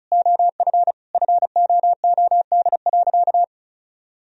Morse code, heard as a keyed tone, about 34 wpm, 700 Hz. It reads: OF FOOD.
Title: Sentences from Top 1000 Words - Review Part 4 - 35wpm